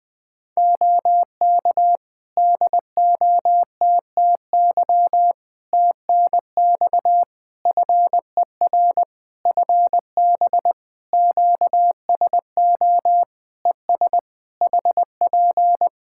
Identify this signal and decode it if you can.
Morse code, heard as a keyed tone, about 20 wpm, 700 Hz. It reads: OK DOTTY TNX FER FB QSO ES HP
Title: Practice QSO - Exchange 4 - Ending - 20wpm